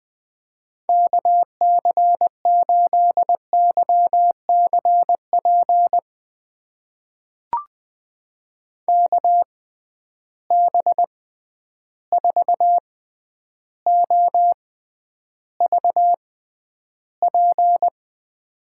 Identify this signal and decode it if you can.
Morse code, heard as a keyed tone, about 20 wpm, spaced as for 10 wpm, 700 Hz. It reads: KC8YCP KB4OVP
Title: USA Call Signs - 6 Characters - 6x Character Spacing - 20wpm